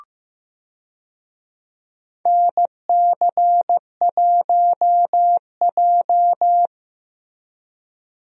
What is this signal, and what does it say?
Morse code, heard as a keyed tone, about 15 wpm, 700 Hz. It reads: NC1J
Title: USA Call Signs - 4 Characters - Repeated 3x - 15wpm